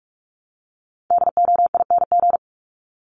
Morse code, heard as a keyed tone, about 44 wpm, 700 Hz. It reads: DOING
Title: ING Suffix - Top 100 Words with Base Words 45wpm